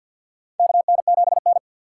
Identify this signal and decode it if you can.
Morse code, heard as a keyed tone, about 52 wpm, 700 Hz. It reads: KN7N
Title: USA Call Signs - 6x Character Spacing - 50wpm